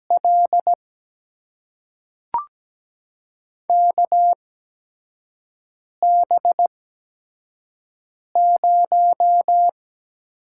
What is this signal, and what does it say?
Morse code, heard as a keyed tone, about 17 wpm, spaced as for 5 wpm, 700 Hz. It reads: L KB0
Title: USA Call Signs - 6 Characters - 8x Character Spacing - 17wpm